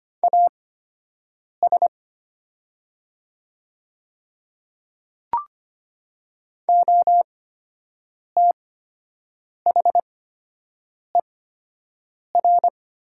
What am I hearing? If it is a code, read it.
Morse code, heard as a keyed tone, about 25 wpm, spaced as for 7 wpm, 700 Hz. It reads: AS OTHER
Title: 8x Character Spacing - Top 100 Words - 25wpm